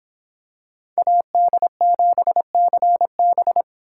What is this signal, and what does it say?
Morse code, heard as a keyed tone, about 26 wpm, 700 Hz. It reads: AD7CB